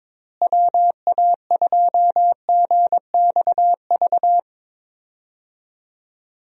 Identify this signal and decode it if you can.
Morse code, heard as a keyed tone, about 22 wpm, 700 Hz. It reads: WA2GXV
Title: USA Call Signs - 6 Characters - Repeated 2x - 22wpm